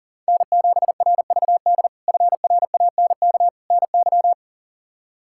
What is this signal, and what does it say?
Morse code, heard as a keyed tone, about 40 wpm, 700 Hz. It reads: N7RVD FRANK NY
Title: K1USN SST QSO Practice - 40wpm